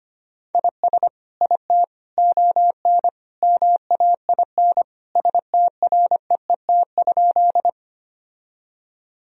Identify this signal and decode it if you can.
Morse code, heard as a keyed tone, about 25 wpm, 700 Hz. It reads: IS IT ON MAIN STREET?
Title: Sentences from Top 1000 Words - Review Part 2 - 25wpm